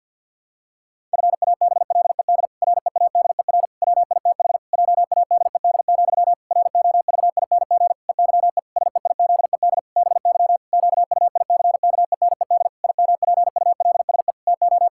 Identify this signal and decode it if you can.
Morse code, heard as a keyed tone, about 46 wpm, 700 Hz. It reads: FABLED READER WITH JADED, ROVING EYE SEIZED BY QUICKENED IMPULSE TO
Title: Pangram Challenge v2 - 50wpm